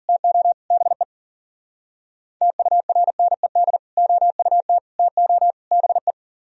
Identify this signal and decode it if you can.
Morse code, heard as a keyed tone, about 49 wpm, 700 Hz. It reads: TO BE TURNED OUT TO BE
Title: Speed-Racing - Sets of 4 Words 40wpm